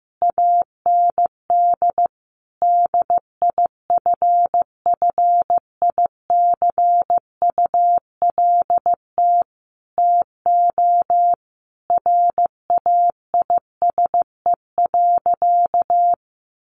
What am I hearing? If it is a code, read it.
Morse code, heard as a keyed tone, about 15 wpm, 700 Hz. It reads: AND DIFFICULT TO RAISE.
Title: Sentences from Top 1400 Words 15wpm